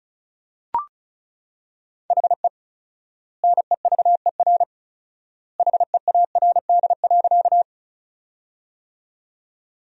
Morse code, heard as a keyed tone, about 36 wpm, 700 Hz. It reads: HE NEVER HEARD.
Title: 4x Word-Spacing - Sentences from Top 400 Words - 35wpm